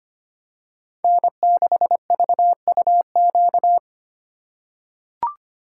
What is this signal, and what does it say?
Morse code, heard as a keyed tone, about 25 wpm, 700 Hz. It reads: N6VUQ
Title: USA Call Signs - 5 Characters - 4x Character Spacing - 25wpm